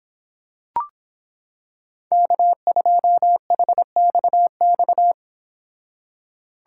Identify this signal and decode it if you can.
Morse code, heard as a keyed tone, about 26 wpm, 700 Hz. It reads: K2HXX